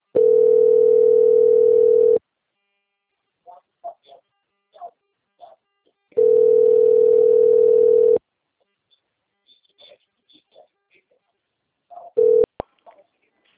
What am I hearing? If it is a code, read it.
Ringback tone (North America), 440 + 480 Hz.
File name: out- - - - - .7446